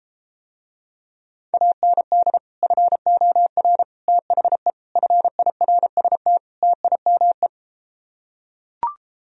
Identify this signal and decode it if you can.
Morse code, heard as a keyed tone, about 33 wpm, 700 Hz. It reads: AND FOR THE FIRST TIME